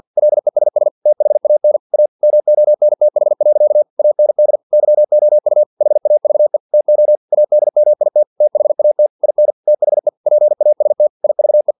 Morse code, heard as a keyed tone, about 50 wpm, 600 Hz. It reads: LESS THAN A MONTH. AND YOU HAVE TO ADMIT THAT IN THE PAST IVE
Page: Never 49wpm